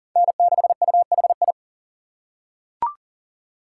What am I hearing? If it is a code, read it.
Morse code, heard as a keyed tone, about 40 wpm, 700 Hz. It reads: N6UHI
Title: USA Call Signs - 6x Character Spacing - 40wpm